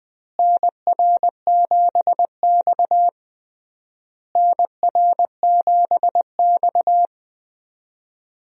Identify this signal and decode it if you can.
Morse code, heard as a keyed tone, about 20 wpm, 700 Hz. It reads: NR7X NR7X
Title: USA Call Signs - 4 Characters - Repeated 3x - 20wpm